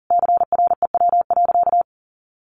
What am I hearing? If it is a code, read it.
Morse code, heard as a keyed tone, about 40 wpm, 700 Hz. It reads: CREW.
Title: Sentences from Top 1500 Words 40wpm